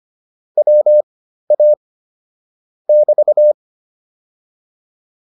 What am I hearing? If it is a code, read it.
Morse code, heard as a keyed tone, about 25 wpm, spaced as for 13 wpm, 600 Hz. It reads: WA <BT>